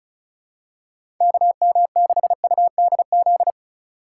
Morse code, heard as a keyed tone, about 34 wpm, 700 Hz. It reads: KM6UDZ
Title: USA Call Signs - 3x Character Spacing - 35wpm